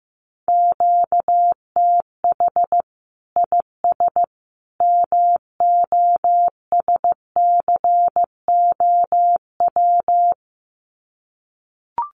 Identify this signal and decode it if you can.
Morse code, heard as a keyed tone, about 14 wpm, 700 Hz. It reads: QTH IS MOSCOW
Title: Common QSO Phrases 15wpm